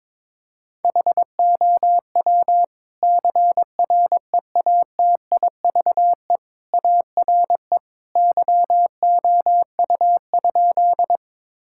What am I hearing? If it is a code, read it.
Morse code, heard as a keyed tone, about 22 wpm, 700 Hz. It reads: HOW CREATIVE ARE YOU?
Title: Sentences from Top 1000 Words - Review Part 5 - 22wpm